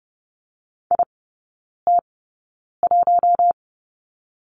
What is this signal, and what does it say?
Morse code, heard as a keyed tone, about 30 wpm, spaced as for 9 wpm, 700 Hz. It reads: IT1
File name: Lesson 19e - Letter w - Three Characters 30wpm